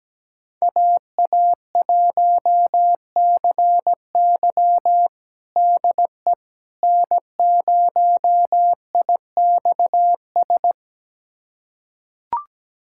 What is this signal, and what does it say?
Morse code, heard as a keyed tone, about 17 wpm, 700 Hz. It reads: AA1CY DE N0IXS
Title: USA Callsign DE Callsign - 5 Characters - 17wpm